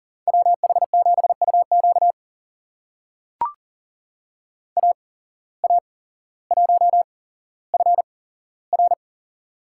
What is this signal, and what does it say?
Morse code, heard as a keyed tone, about 40 wpm, spaced as for 17 wpm, 700 Hz. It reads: WH7UQ AA1FR
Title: USA Call Signs - 5 Characters - 8x Character Spacing - 40wpm